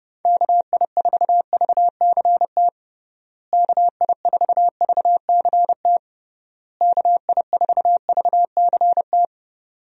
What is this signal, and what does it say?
Morse code, heard as a keyed tone, about 30 wpm, 700 Hz. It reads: KI4VCT KI4VCT KI4VCT
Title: USA Call Signs - 6 Characters - Repeated 3x - 30wpm